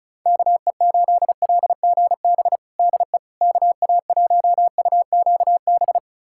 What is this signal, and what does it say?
Morse code, heard as a keyed tone, about 36 wpm, 700 Hz. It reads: KE8LGB DE KA1UQB
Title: USA Callsign DE Callsign - 6 Characters - 35wpm